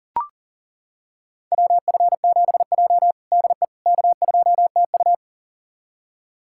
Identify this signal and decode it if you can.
Morse code, heard as a keyed tone, about 40 wpm, 700 Hz. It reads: WF7J DE K2TU
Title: USA Callsign DE Callsign - 4 Characters - 40wpm